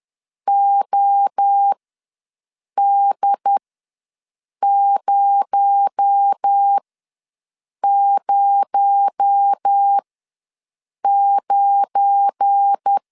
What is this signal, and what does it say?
Morse code, heard as a keyed tone, about 10 wpm, spaced as for 6 wpm, 800 Hz. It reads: OD009
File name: LEZIONI / 8 YJ45 / 8 YJ45 ws 10wpm 3843